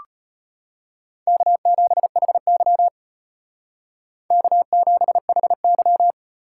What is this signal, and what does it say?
Morse code, heard as a keyed tone, about 30 wpm, 700 Hz. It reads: K7HY K7HY
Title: Speed Racing - USA 1x2 Call Signs 25wpm